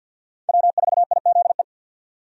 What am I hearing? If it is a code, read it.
Morse code, heard as a keyed tone, about 48 wpm, 700 Hz. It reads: W4IZE
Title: USA Call Signs - 5 Characters - 50wpm